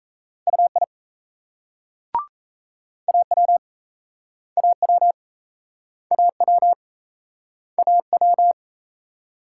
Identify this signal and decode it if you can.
Morse code, heard as a keyed tone, about 36 wpm, 700 Hz. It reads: UI AW AW AW AW